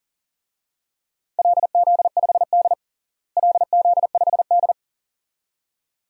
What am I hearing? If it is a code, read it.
Morse code, heard as a keyed tone, about 40 wpm, 700 Hz. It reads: LZ5D LZ5D
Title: DXCC-100 Call Signs - 4 Characters - Repeated 2x - 40wpm